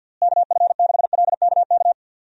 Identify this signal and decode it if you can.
Morse code, heard as a keyed tone, about 46 wpm, 700 Hz. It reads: KF6LKK
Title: USA Call Signs - 6 Characters - Repeated 2x - 50wpm